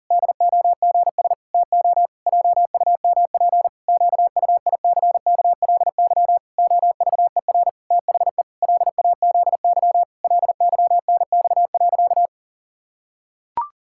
Morse code, heard as a keyed tone, about 40 wpm, 700 Hz. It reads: DOGS TO JUMP QUICKLY OVER THE LAZY LYNX.
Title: Pangram Challenge v2 - 40wpm